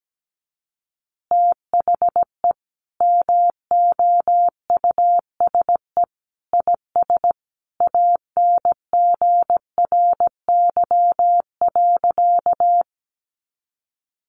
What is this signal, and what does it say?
Morse code, heard as a keyed tone, about 17 wpm, 700 Hz. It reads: THE MOUSE IS ANGRY.
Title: Easy Phrases 17wpm